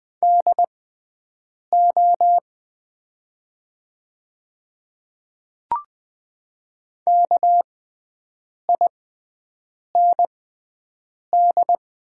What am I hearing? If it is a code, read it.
Morse code, heard as a keyed tone, about 20 wpm, spaced as for 7 wpm, 700 Hz. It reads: DO KIND
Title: 6x Character Spacing - Top 200 Words - 20wpm